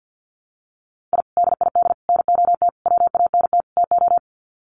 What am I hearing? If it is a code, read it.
Morse code, heard as a keyed tone, about 50 wpm, 700 Hz. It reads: I DID NOT WANT TO
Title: Sets of 5 Words 50wpm